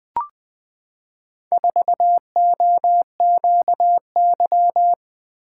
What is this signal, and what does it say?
Morse code, heard as a keyed tone, about 21 wpm, 700 Hz. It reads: 4OQY
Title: Four Letter-Number Combinations - 20wpm